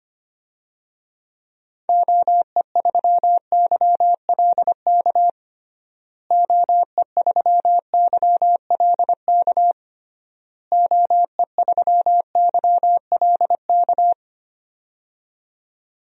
Morse code, heard as a keyed tone, about 25 wpm, 700 Hz. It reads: OE3YLK OE3YLK OE3YLK
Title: DXCC-100 Call Signs - 6 Characters - Repeated 3x - 25wpm